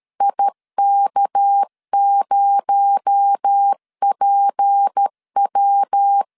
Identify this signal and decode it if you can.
Morse code, heard as a keyed tone, about 13 wpm, 800 Hz. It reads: IK0PW
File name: LEZIONI / 6 PKX0 / 6 PKX0 12wpm 2231